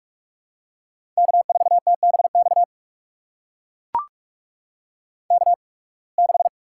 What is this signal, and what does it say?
Morse code, heard as a keyed tone, about 44 wpm, spaced as for 29 wpm, 700 Hz. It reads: K4TBX K6
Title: USA Call Signs - 8x Character Spacing - 45wpm